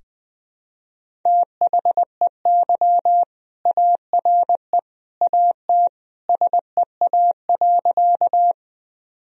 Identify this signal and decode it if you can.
Morse code, heard as a keyed tone, about 20 wpm, 700 Hz. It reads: THEY ARE AT SEA.
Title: Sentences from Top 1000 Words - Review Part 1 - 20wpm